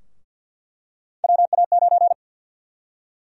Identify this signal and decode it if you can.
Morse code, heard as a keyed tone, about 48 wpm, 700 Hz. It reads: WA9
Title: USA Call Sign Prefixes - 50wpm